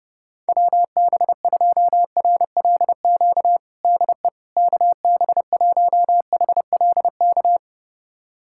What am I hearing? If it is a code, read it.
Morse code, heard as a keyed tone, about 30 wpm, 700 Hz. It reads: WB2RLQ DE KB1HLK
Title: USA Callsign DE Callsign - 6 Characters - 30wpm